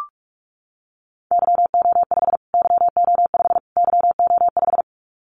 Morse code, heard as a keyed tone, about 44 wpm, 700 Hz. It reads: YO5 YO5 YO5
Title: Callsign Prefixes for Top 25 DXCC - Repeated 3x - 45wpm